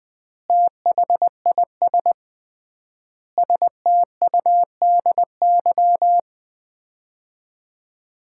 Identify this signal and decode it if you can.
Morse code, heard as a keyed tone, about 20 wpm, 700 Hz. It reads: THIS STUDY
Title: 3x Word-Spacing - Sets of 5 Words - 20wpm